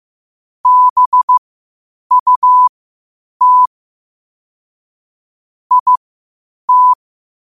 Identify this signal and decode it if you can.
Morse code, heard as a keyed tone, about 15 wpm, spaced as for 8 wpm, 1000 Hz. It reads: BUT IT
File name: qotdfiles / 8wpm